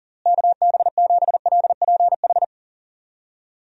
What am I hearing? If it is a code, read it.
Morse code, heard as a keyed tone, about 40 wpm, 700 Hz. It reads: KB7LPH
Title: USA Call Signs - 6 Characters - Repeated 3x - 40wpm